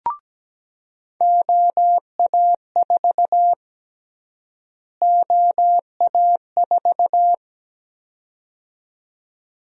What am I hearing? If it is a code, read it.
Morse code, heard as a keyed tone, about 17 wpm, 700 Hz. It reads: OA4 OA4
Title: USA Call Sign Prefixes - Repeated 2x - 17wpm